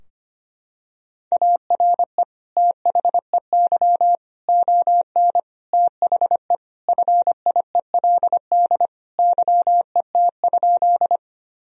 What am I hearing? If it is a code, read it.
Morse code, heard as a keyed tone, about 25 wpm, 700 Hz. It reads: ARE THEY ON THE FIELD YET?
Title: Sentences from Top 1000 Words - Review Part 2 - 25wpm